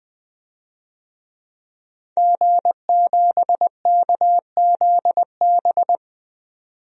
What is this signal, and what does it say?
Morse code, heard as a keyed tone, about 20 wpm, 700 Hz. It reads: G7KZB
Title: DXCC-100 Call Signs - 5 Characters - 20wpm